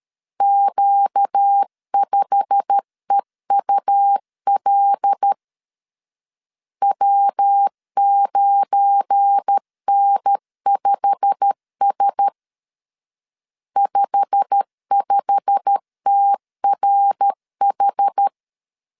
Morse code, heard as a keyed tone, about 13 wpm, 800 Hz. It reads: Q5EUL W9N5S 55TRH
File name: LEZIONI / 8 YJ45 / 8 YJ45 12wpm 2151